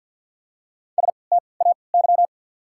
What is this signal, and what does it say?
Morse code, heard as a keyed tone, about 48 wpm, spaced as for 28 wpm, 700 Hz. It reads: STAY
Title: 3x Character Spacing - Top 500 Words - 50wpm